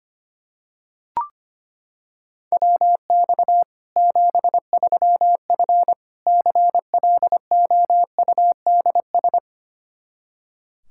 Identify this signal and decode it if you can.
Morse code, heard as a keyed tone, about 25 wpm, 700 Hz. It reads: WX 73F CLOUDS
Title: Practice QSO - Exchange 2 - Info - 25wpm